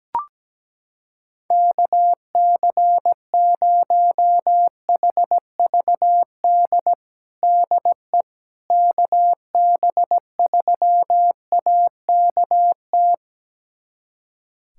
Morse code, heard as a keyed tone, about 17 wpm, 700 Hz. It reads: KC0HVD DE KB3AKT
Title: USA Callsign DE Callsign 17wpm